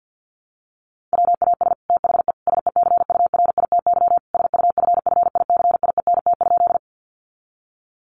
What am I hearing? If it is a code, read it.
Morse code, heard as a keyed tone, about 50 wpm, 700 Hz. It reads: WAS THE SECURITY SUFFICIENT?
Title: Sentences from Top 1400 Words 50wpm